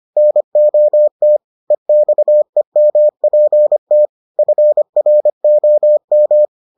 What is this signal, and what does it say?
Morse code, heard as a keyed tone, about 25 wpm, 600 Hz. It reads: NOT EXEMPT FROM